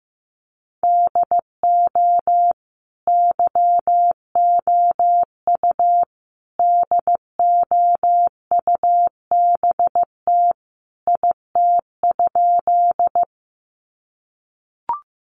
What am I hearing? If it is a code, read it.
Morse code, heard as a keyed tone, about 15 wpm, 700 Hz. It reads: DO YOU DOUBT IT?
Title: Sentences from Top 1500 Words 15wpm